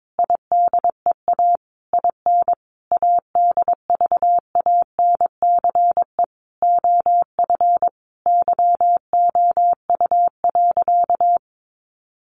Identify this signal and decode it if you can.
Morse code, heard as a keyed tone, about 22 wpm, 700 Hz. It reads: IDEA IN ADVANCE OF YOU.
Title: Sentences from Top 1400 Words 22wpm